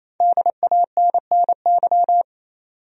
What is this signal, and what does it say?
Morse code, heard as a keyed tone, about 28 wpm, 700 Hz. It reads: DANNY
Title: USA Common Names - Max 5 Letters - 28wpm